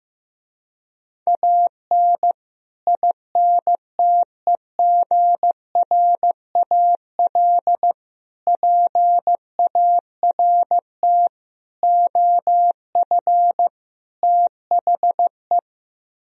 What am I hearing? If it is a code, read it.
Morse code, heard as a keyed tone, about 15 wpm, 700 Hz. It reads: AN INTEGRAL PART OF THE
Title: Speed-Racing - Sets of 5 Words 15wpm